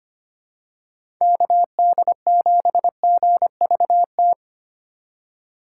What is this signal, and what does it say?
Morse code, heard as a keyed tone, about 25 wpm, 700 Hz. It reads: KD7GVT
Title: USA Call Signs - 6 Characters - 4x Character Spacing - 25wpm